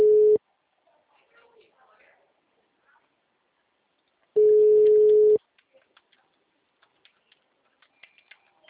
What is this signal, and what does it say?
Ringback tone (Europe, 425 Hz), with speech.